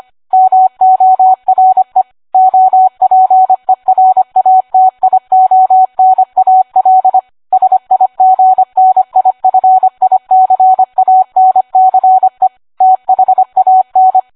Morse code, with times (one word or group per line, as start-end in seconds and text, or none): 0.3-2.0 MORE
2.3-7.2 OPERATIONAL
7.5-12.5 SIGNIFICANCE
12.8-14.2 THAN